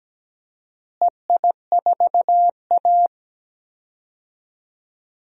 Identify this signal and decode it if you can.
Morse code, heard as a keyed tone, about 17 wpm, 700 Hz. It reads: EI4A
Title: DXCC-100 Call Signs - 4 Characters - Repeated 2x - 17wpm